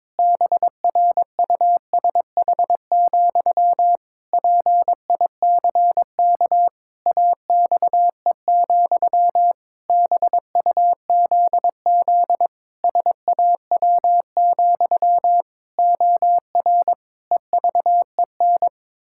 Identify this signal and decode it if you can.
Morse code, heard as a keyed tone, about 22 wpm, 700 Hz. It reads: BRUSH, PICK AXE, BUZZ SAW, OR EVEN
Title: Pangram Challenge v2 - 22wpm